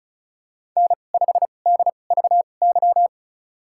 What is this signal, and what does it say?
Morse code, heard as a keyed tone, about 36 wpm, spaced as for 25 wpm, 700 Hz. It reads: N5DVY
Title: USA Call Signs - 5 Characters - 2x Character Spacing - 35wpm